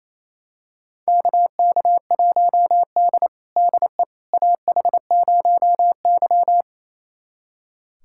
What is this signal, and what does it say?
Morse code, heard as a keyed tone, about 28 wpm, 700 Hz. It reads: KK1D DE AH0Y
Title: USA Callsign DE Callsign - 4 Characters - 28wpm